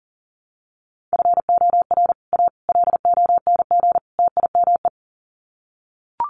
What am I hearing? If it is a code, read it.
Morse code, heard as a keyed tone, about 40 wpm, 700 Hz. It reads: FOR A LONG TIME
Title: Sets of 4 Words 40wpm